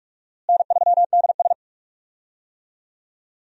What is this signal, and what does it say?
Morse code, heard as a keyed tone, about 46 wpm, 700 Hz. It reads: N3DS
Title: USA Call Signs - 4 Characters - Repeated 2x - 45wpm